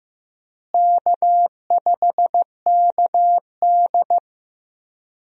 Morse code, heard as a keyed tone, about 15 wpm, 700 Hz. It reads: K5KD
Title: USA Call Signs - 4 Characters - Repeated 2x - 15wpm